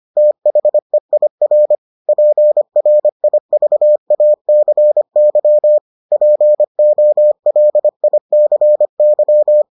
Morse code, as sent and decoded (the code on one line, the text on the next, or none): - .... .. .-. / .--. .-. .. ...- .- -.-. -.-- / .--. --- .-.. .. -.-. -.--
THEIR PRIVACY POLICY